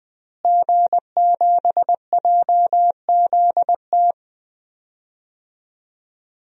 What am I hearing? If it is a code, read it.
Morse code, heard as a keyed tone, about 20 wpm, 700 Hz. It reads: G7JZT
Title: DXCC-100 Call Signs - 5 Characters - Repeated 2x - 20wpm